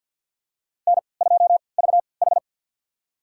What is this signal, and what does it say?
Morse code, heard as a keyed tone, about 50 wpm, spaced as for 28 wpm, 700 Hz. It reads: N2VH
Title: USA Call Signs - 4 Characters - 3x Character Spacing - 50wpm